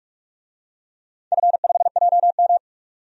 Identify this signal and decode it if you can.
Morse code, heard as a keyed tone, about 44 wpm, 700 Hz. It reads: F5JM
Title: DXCC-100 Call Signs - 4 Characters - 45wpm